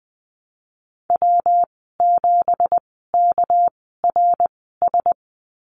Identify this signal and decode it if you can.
Morse code, heard as a keyed tone, about 20 wpm, spaced as for 15 wpm, 700 Hz. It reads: W7KRS
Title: 231 Lesson 37g - Letter x - Call Signs 20wpm